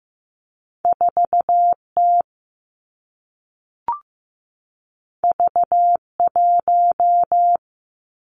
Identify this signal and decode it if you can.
Morse code, heard as a keyed tone, about 15 wpm, 700 Hz. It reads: 4 T V 1
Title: Two Challenging Letter-Number Combinations 15wpm